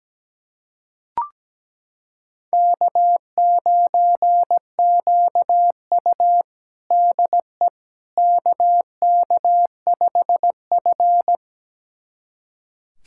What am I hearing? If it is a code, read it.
Morse code, heard as a keyed tone, about 17 wpm, 700 Hz. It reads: K9QU DE KK5F
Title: USA Callsign DE Callsign - 4 Characters - 17wpm